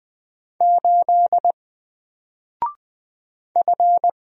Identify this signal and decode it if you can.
Morse code, heard as a keyed tone, about 20 wpm, 700 Hz. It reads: 8 F
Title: Single Character (Letter:Number:Punctuation:Prosign) 20wpm